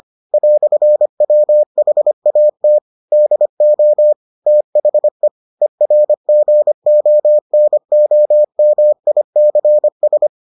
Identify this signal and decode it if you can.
Morse code, heard as a keyed tone, about 25 wpm, 600 Hz. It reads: "WHAT DO THE ERGONOMICS